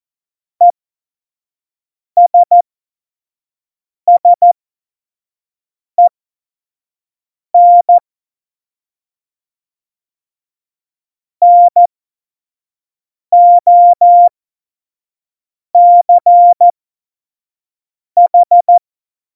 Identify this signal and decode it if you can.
Morse code, heard as a keyed tone, about 14 wpm, spaced as for 5 wpm, 700 Hz. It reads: ESSEN NOCH
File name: Morsekurs / Witze / Witz 18. September 2018, 25 F70